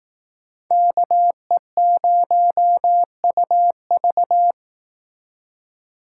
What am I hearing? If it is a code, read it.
Morse code, heard as a keyed tone, about 18 wpm, 700 Hz. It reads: KE0UV
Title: Speed Racing - USA 2x2 Call Signs 15wpm